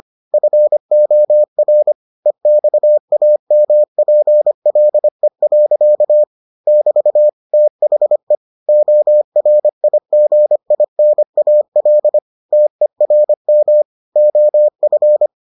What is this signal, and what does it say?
Morse code, heard as a keyed tone, about 25 wpm, 600 Hz. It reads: FOR EXAMPLE. <BT> THE ORIGINAL TERM OF